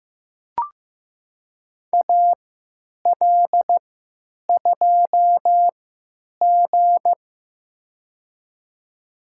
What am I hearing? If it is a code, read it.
Morse code, heard as a keyed tone, about 15 wpm, spaced as for 9 wpm, 700 Hz. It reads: AL2G
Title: USA Call Signs - 4 Characters - 3x Character Spacing - 15wpm